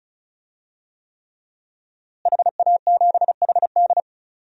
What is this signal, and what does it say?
Morse code, heard as a keyed tone, about 34 wpm, 700 Hz. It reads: HA7HD
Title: DXCC-100 Call Signs - 5 Characters - Repeated 3x - 35wpm